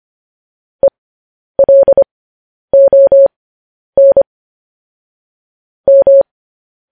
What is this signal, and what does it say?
Morse code, heard as a keyed tone, about 25 wpm, spaced as for 10 wpm, 550 Hz. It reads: ELON M
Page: news-headlines-in-morse-code-at-10-wpm